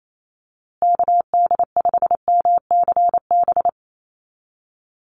Morse code, heard as a keyed tone, about 28 wpm, 690 Hz. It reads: KD5MCB
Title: USA Call Signs Repeated 3x 28wpm